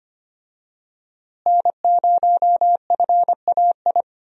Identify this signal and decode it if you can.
Morse code, heard as a keyed tone, about 25 wpm, 700 Hz. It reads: N0FAI
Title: USA Call Signs - 5 Characters - Repeated 3x - 25wpm